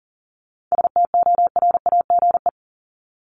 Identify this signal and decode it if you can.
Morse code, heard as a keyed tone, about 40 wpm, 700 Hz. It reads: STORAGE
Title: Top 1251-1500 Words 40wpm